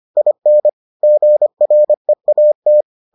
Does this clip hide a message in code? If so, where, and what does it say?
Yes: Morse code from 0.2 to 2.8 s, heard as a keyed tone, about 25 wpm, 600 Hz. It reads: IN GREAT